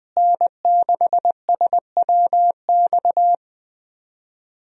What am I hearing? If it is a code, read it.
Morse code, heard as a keyed tone, about 21 wpm, 700 Hz. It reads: N6SWX
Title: USA Call Signs - 5 Characters - Repeated 2x - 20wpm